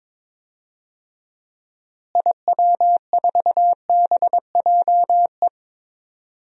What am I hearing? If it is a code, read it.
Morse code, heard as a keyed tone, about 22 wpm, 700 Hz. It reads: IW4BJE
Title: DXCC-100 Call Signs - 6 Characters - Repeated 2x - 22wpm